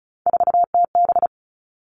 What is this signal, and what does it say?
Morse code, heard as a keyed tone, about 34 wpm, 700 Hz. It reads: 4TB
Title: All Cut Numbers - 3-Digit 35wpm